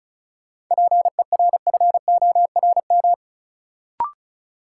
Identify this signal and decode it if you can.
Morse code, heard as a keyed tone, about 36 wpm, 700 Hz. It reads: PERFORM
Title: Sentences from Top 1000 Words - Review Part 5 - New Word Drill - 35wpm